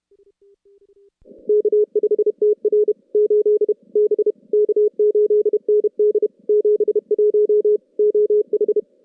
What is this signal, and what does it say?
Morse code, heard as a keyed tone, about 34 wpm, 410 Hz. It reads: K5TR 8 B K8ND 71 OH
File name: contests / CW-SS-k5tr- -2100